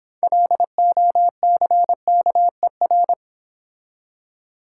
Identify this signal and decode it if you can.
Morse code, heard as a keyed tone, about 26 wpm, 700 Hz. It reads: LOCKER